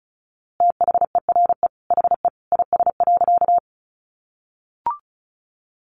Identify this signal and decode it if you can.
Morse code, heard as a keyed tone, about 34 wpm, 700 Hz. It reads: THERE HE IS.
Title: A Sound Of Thunder 35wpm